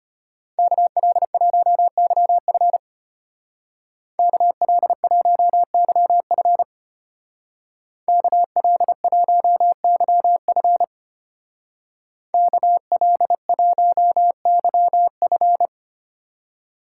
Morse code, heard as a keyed tone, about 30 wpm, 700 Hz. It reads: KL1YF KL1YF KL1YF KL1YF
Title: Speed Racing - USA Call Signs 25wpm